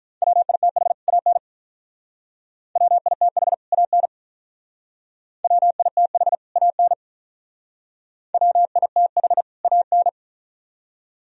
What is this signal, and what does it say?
Morse code, heard as a keyed tone, about 45 wpm, 700 Hz. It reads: WITH AN WITH AN WITH AN WITH AN